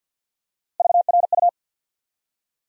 Morse code, heard as a keyed tone, about 51 wpm, 700 Hz. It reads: VRU
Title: USA Call Sign Postfixes - 3 Letters 50wpm